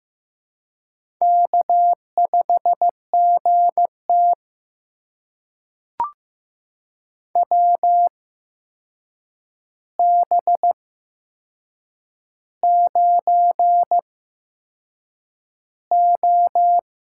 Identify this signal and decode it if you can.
Morse code, heard as a keyed tone, about 15 wpm, spaced as for 6 wpm, 700 Hz. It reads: K5GT WB9O
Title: USA Call Signs - 4 Characters - 8x Character Spacing - 15wpm